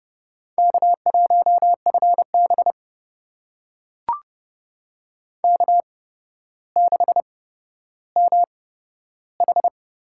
Morse code, heard as a keyed tone, about 32 wpm, spaced as for 13 wpm, 700 Hz. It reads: K1FB K6MH
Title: USA Call Signs - 4 Characters - 8x Character Spacing - 30wpm